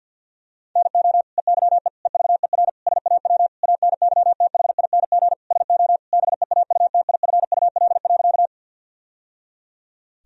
Morse code, heard as a keyed tone, about 50 wpm, 700 Hz. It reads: NO EYE EVER SAW ANYTHING SO BEAUTIFUL.
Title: Sentences from Top 1000 Words - Review Part 1 - 50wpm